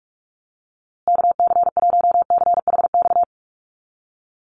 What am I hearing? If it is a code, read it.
Morse code, heard as a keyed tone, about 44 wpm, 700 Hz. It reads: KC1CHX
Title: USA Call Signs Repeated 3x 45wpm